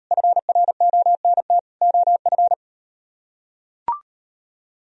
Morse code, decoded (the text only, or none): FRONT OF